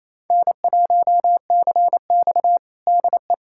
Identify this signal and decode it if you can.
Morse code, heard as a keyed tone, about 28 wpm, 700 Hz. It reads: N1CX DE
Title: USA Callsign DE Callsign - 4 Characters - 28wpm